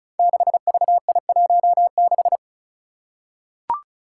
Morse code, heard as a keyed tone, about 36 wpm, 700 Hz. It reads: 6VI16
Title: Five Challenging Letter-Number Combinations 35wpm